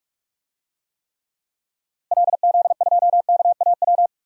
Morse code, heard as a keyed tone, about 46 wpm, 700 Hz. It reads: LZ2KAW
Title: DXCC-100 Call Signs - 6 Characters - Repeated 2x - 45wpm